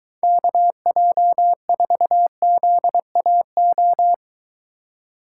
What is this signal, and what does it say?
Morse code, heard as a keyed tone, about 22 wpm, 700 Hz. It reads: KJ4ZAO